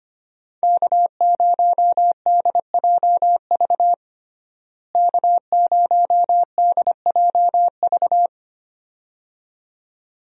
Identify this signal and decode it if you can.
Morse code, heard as a keyed tone, about 25 wpm, 700 Hz. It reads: K0DJV K0DJV
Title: USA Call Signs - 5 Characters - Repeated 2x - 25wpm